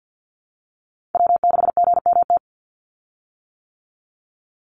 Morse code, heard as a keyed tone, about 50 wpm, 700 Hz. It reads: W6GMT
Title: Call Signs - 50wpm